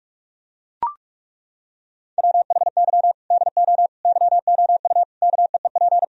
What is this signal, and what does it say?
Morse code, heard as a keyed tone, about 41 wpm, 700 Hz. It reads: WHY DO YOU KEEP
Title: Sentences from Top 1000 Words - Review Part 3 - 45wpm